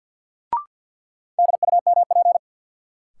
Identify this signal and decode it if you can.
Morse code, heard as a keyed tone, about 50 wpm, 700 Hz. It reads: DUMP
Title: Sentences from Top 1300 Words - New Word Drill - 50wpm